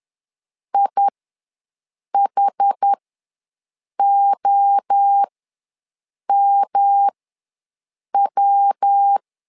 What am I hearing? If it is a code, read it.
Morse code, heard as a keyed tone, about 10 wpm, spaced as for 6 wpm, 800 Hz. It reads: IHOMW
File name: LEZIONI / 3 MWRO / 3 MWRO ws 10wpm 2310